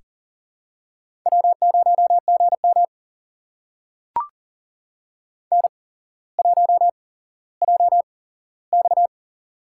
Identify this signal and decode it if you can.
Morse code, heard as a keyed tone, about 40 wpm, spaced as for 17 wpm, 700 Hz. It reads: W0GM N1JX
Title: USA Call Signs - 4 Characters - 8x Character Spacing - 40wpm